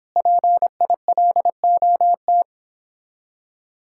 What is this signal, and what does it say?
Morse code, heard as a keyed tone, about 29 wpm, 700 Hz. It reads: PILOT